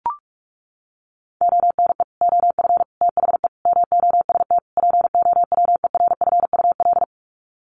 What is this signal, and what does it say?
Morse code, heard as a keyed tone, about 44 wpm, 700 Hz. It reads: ONE OF THE MOST POWERFUL
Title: Sets of 5 Words Encore 45wpm